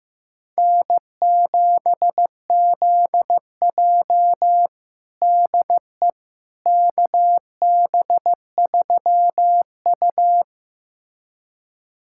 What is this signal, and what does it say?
Morse code, heard as a keyed tone, about 15 wpm, 700 Hz. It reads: N7ZJ DE KB3U
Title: USA Callsign DE Callsign - 4 Characters - 15wpm